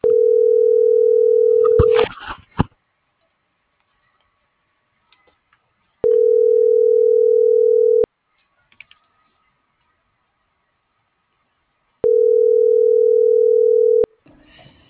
Ringback tone (North America), 440 + 480 Hz.